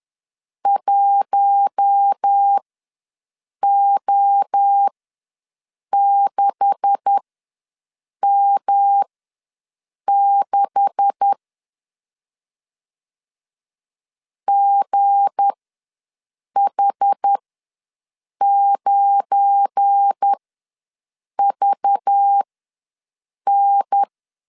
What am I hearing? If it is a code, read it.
Morse code, heard as a keyed tone, about 10 wpm, spaced as for 6 wpm, 800 Hz. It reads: 1O6M6 GH9VN
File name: LEZIONI / 9 286 ws 10wpm 4032